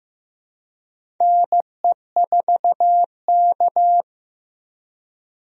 Morse code, heard as a keyed tone, about 15 wpm, 700 Hz. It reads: NE4K
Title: USA Call Signs - 4 Characters - Repeated 3x - 15wpm